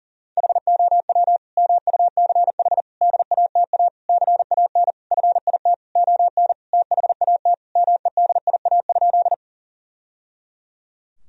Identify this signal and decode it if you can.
Morse code, heard as a keyed tone, about 40 wpm, 700 Hz. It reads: HOW MUCH DATA CAN FIT ON THAT MEDIA?
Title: Sentences from Top 1000 Words - Review Part 4 - 40wpm